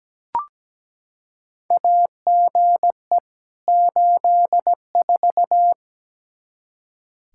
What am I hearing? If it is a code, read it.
Morse code, heard as a keyed tone, about 17 wpm, 700 Hz. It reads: AGE 84
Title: Practice QSO - Exchange 3 - More Info - 17wpm